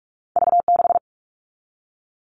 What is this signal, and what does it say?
Morse code, heard as a keyed tone, about 46 wpm, 700 Hz. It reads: V6
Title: All Cut Numbers - 2-Digit 45wpm